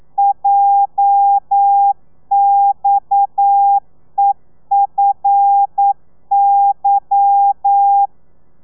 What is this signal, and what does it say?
Morse code, heard as a keyed tone, about 9 wpm, 800 Hz. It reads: JXEFY
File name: telegrafia / i2tfj / Lez11 40